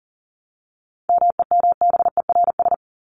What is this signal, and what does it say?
Morse code, heard as a keyed tone, about 40 wpm, 700 Hz. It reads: MEMBERS
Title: ES:S Suffix - Top 100 Words with Base Words 40wpm